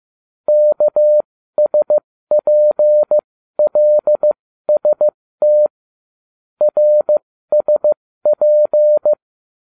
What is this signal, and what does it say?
Morse code, heard as a keyed tone, about 15 wpm, spaced as for 13 wpm, 600 Hz. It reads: KSPLST RSP